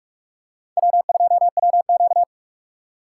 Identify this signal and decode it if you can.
Morse code, heard as a keyed tone, about 44 wpm, 700 Hz. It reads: W2WQ